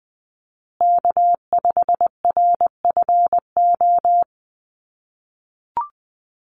Morse code, heard as a keyed tone, about 20 wpm, 700 Hz. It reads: K5RFO
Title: USA Call Signs Repeated 2x 20wpm